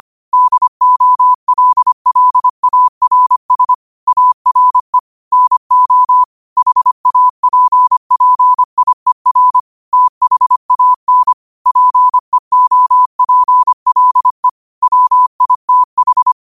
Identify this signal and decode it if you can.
Morse code, heard as a keyed tone, about 25 wpm, 1000 Hz. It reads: DOLLARS ARE NO HAPPIER THAN PEOPLE WITH